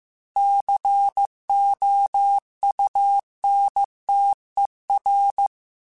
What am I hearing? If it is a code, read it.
Morse code, heard as a keyed tone, about 15 wpm, 800 Hz. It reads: COUNTER
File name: Technical Subjects / Technical Subjects-mp3s / XL @13